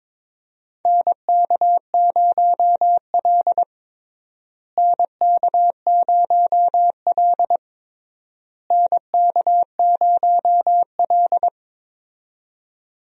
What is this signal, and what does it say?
Morse code, heard as a keyed tone, about 22 wpm, 700 Hz. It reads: NK0L NK0L NK0L
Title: USA Call Signs - 4 Characters - Repeated 3x - 22wpm